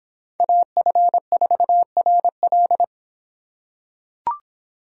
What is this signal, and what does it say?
Morse code, heard as a keyed tone, about 26 wpm, 700 Hz. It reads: AF4RL